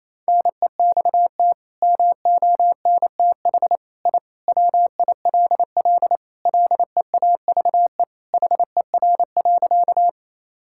Morse code, heard as a keyed tone, about 28 wpm, 700 Hz. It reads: NEXT MONTH I WILL LEAVE HER.
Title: Sentences from Top 1000 Words - Review Part 5 - 28wpm